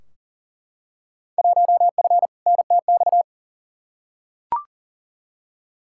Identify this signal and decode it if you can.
Morse code, heard as a keyed tone, about 40 wpm, 700 Hz. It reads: 1F NTX
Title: Field Day Exchanges Encore - 40wpm